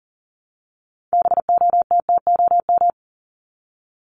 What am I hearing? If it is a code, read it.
Morse code, heard as a keyed tone, about 40 wpm, 700 Hz. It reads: BOTTOM
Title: Top 751-1000 Words 40wpm